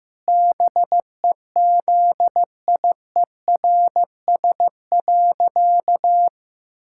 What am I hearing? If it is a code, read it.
Morse code, heard as a keyed tone, about 15 wpm, 700 Hz. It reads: BEZIERS.
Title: Pangram Challenge v2 - 15wpm